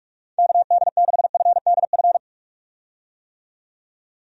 Morse code, heard as a keyed tone, about 46 wpm, 700 Hz. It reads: KD6FDF
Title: USA Call Signs - 6 Characters - 45wpm